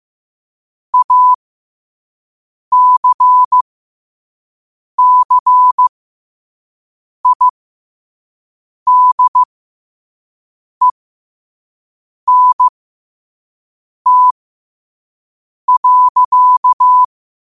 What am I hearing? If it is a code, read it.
Morse code, heard as a keyed tone, about 15 wpm, spaced as for 5 wpm, 1000 Hz. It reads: ACCIDENT.
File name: qotdfiles / 5wpm